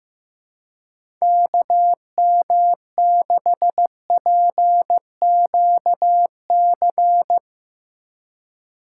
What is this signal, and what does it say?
Morse code, heard as a keyed tone, about 15 wpm, 700 Hz. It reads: KM6PQC
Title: USA Call Signs - 6 Characters - 3x Character Spacing - 15wpm